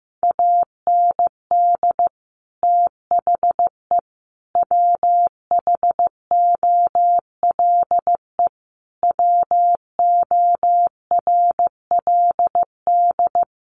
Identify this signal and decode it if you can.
Morse code, heard as a keyed tone, about 15 wpm, 700 Hz. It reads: AND THE WHOLE WORLD
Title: Sentences from Top 1300 Words 15wpm